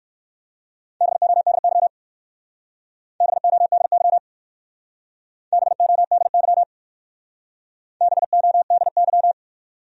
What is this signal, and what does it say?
Morse code, heard as a keyed tone, about 55 wpm, 700 Hz. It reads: BODY BODY BODY BODY
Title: Speed-Racing Top 500 Words 45wpm